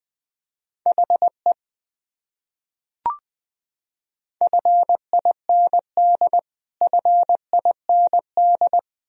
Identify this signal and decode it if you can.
Morse code, heard as a keyed tone, about 20 wpm, 700 Hz. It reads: HE FIND FIND
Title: Top 100 Words Repeated 20wpm